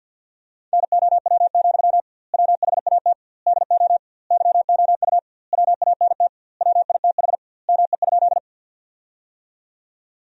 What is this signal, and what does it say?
Morse code, heard as a keyed tone, about 50 wpm, 700 Hz. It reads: NOW, WHAT DO YOU WANT WITH ME?
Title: 2x Word-Spacing - Sentences from Top 200 Words - 50wpm